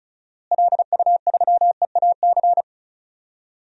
Morse code, heard as a keyed tone, about 36 wpm, 700 Hz. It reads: LU3EAC
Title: DXCC-100 Call Signs - 6 Characters - Repeated 2x - 35wpm